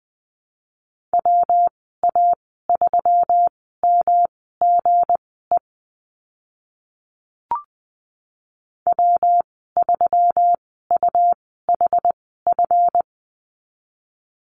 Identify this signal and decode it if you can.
Morse code, heard as a keyed tone, about 20 wpm, spaced as for 15 wpm, 700 Hz. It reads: WA3MGE W3UHF
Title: 193 Lesson 31g - Letter v - Call Signs 20wpm